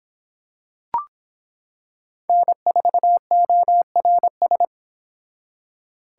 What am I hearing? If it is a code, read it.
Morse code, heard as a keyed tone, about 25 wpm, 700 Hz. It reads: N4ORS